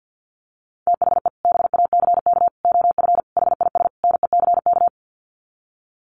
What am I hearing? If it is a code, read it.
Morse code, heard as a keyed tone, about 50 wpm, 700 Hz. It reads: THE BACK OF HIS NECK
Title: Sets of 5 Words Encore 50wpm